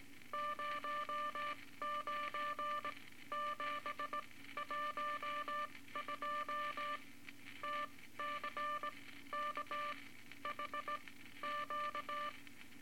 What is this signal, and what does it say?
Morse code, heard as a keyed tone, about 19 wpm, spaced as for 13 wpm, 1180 Hz, with noise. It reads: 09712 TCKHQ
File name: Morse Code Records / 18 A LCM 164 Ten Words Per Minute